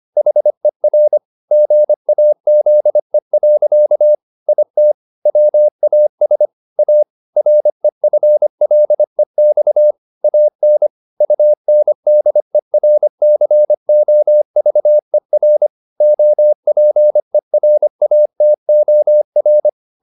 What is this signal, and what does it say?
Morse code, heard as a keyed tone, about 25 wpm, 600 Hz. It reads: HER GAZE. IT WAS A REFLEX AN UNDERCOVER OPERATOR